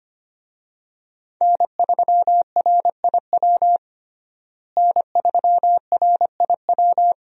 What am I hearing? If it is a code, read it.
Morse code, heard as a keyed tone, about 25 wpm, 700 Hz. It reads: N3RIW N3RIW
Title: USA Call Signs - 5 Characters - Repeated 3x - 25wpm